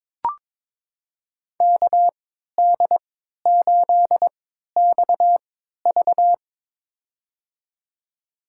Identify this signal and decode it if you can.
Morse code, heard as a keyed tone, about 21 wpm, spaced as for 12 wpm, 700 Hz. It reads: KD8XV
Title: USA Call Signs - 5 Characters - 3x Character Spacing - 22wpm